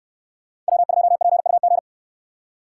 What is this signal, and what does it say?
Morse code, heard as a keyed tone, about 69 wpm, 700 Hz. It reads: W2PUG